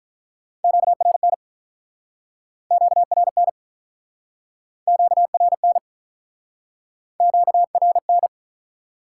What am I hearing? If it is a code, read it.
Morse code, heard as a keyed tone, about 47 wpm, 700 Hz. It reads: QRN QRN QRN QRN